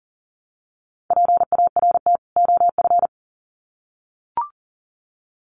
Morse code, heard as a keyed tone, about 40 wpm, 700 Hz. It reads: PART OF
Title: Sets of 2 Words 40wpm